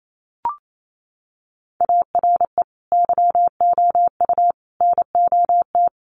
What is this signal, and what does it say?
Morse code, heard as a keyed tone, about 28 wpm, 700 Hz. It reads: ARE YOU NOT
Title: Sentences from Top 600 Words 28wpm